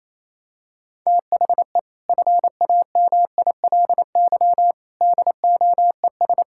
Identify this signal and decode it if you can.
Morse code, heard as a keyed tone, about 28 wpm, 700 Hz. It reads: THE FAMILY DOES
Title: Sentences from Top 1600 Words 28wpm